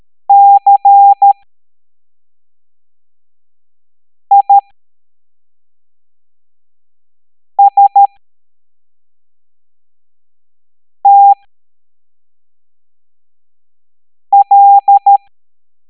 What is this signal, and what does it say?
Morse code, heard as a keyed tone, about 13 wpm, spaced as for 3 wpm, 800 Hz. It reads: CISTL
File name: hamdata / L09-EX04